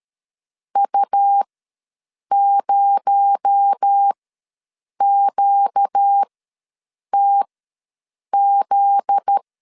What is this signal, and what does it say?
Morse code, heard as a keyed tone, about 13 wpm, spaced as for 7 wpm, 800 Hz. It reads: U0QTZ
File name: LEZIONI / 8 YJ45 / 8 YJ ws 12wpm 3342